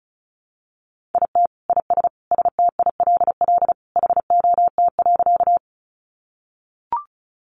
Morse code, heard as a keyed tone, about 34 wpm, 700 Hz. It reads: IT IS STILL HOT.
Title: Additional Easy Phrases 35wpm